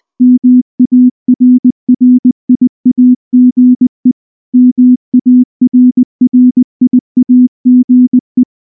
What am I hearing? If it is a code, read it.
Morse code, heard as a keyed tone, about 19 wpm, 250 Hz. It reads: MARRIAGE MARRIAGE
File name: morse / morse-training-20WPM-10